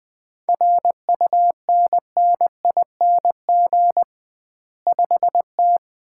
Morse code, heard as a keyed tone, about 21 wpm, 700 Hz. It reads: RUNNING 5T
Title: Common QSO Phrases - 2x Word-Spacing - 20wpm